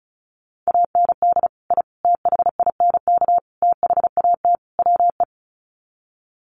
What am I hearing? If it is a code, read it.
Morse code, heard as a keyed tone, about 34 wpm, 700 Hz. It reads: AND I THINK THAT WE
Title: Sets of 5 Words Encore2 35wpm